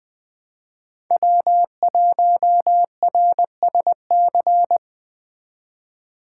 Morse code, heard as a keyed tone, about 20 wpm, 700 Hz. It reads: W1RSC
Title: Speed Racing - USA 1x3 Call Signs 17wpm